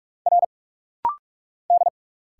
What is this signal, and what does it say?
Morse code, heard as a keyed tone, about 46 wpm, 700 Hz. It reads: R D
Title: Single Letters - Rapid-Fire 45wpm